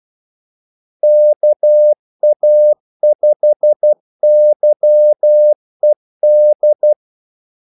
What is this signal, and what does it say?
Morse code, heard as a keyed tone, about 12 wpm, 600 Hz. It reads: KA5YED
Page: 2 ekstra ordmellemrum mellem kaldesignalerne